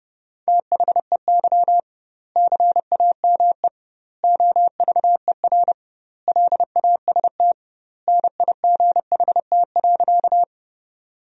Morse code, heard as a keyed tone, about 30 wpm, 700 Hz. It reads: THEY CAME OVER LAST NIGHT.
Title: 2x Word-Spacing - Sentences from Top 300 Words - 30wpm